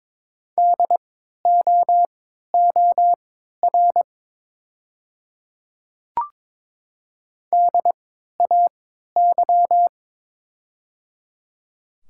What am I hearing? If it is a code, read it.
Morse code, heard as a keyed tone, about 23 wpm, spaced as for 13 wpm, 700 Hz. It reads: DOOR DAY
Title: 3x Character Spacing - Top 300 Words - 22wpm